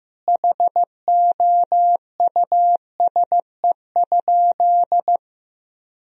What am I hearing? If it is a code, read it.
Morse code, heard as a keyed tone, about 16 wpm, 700 Hz. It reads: HOUSE?
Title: Sentences from Top 1000 Words - Review Part 4 - 15wpm